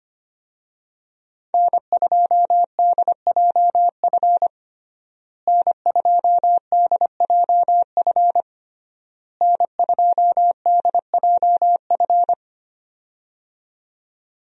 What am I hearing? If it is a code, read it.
Morse code, heard as a keyed tone, about 25 wpm, 700 Hz. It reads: N2DJF N2DJF N2DJF
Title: USA Call Signs - 5 Characters - Repeated 3x - 25wpm